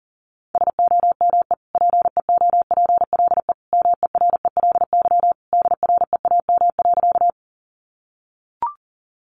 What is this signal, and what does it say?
Morse code, heard as a keyed tone, about 40 wpm, 700 Hz. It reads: SOME PEOPLE MERELY DREAM.
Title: Sentences from Top 1500 Words 40wpm